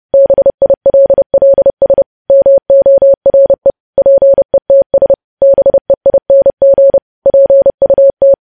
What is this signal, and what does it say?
Morse code, heard as a keyed tone, about 30 wpm, 550 Hz. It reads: BILLS MORE PETS BEING PUT
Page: news headlines in morse code at 30 wpm